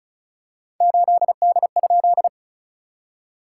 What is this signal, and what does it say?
Morse code, heard as a keyed tone, about 36 wpm, 690 Hz. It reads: 8D?
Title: SOTA QSO Practice 35wpm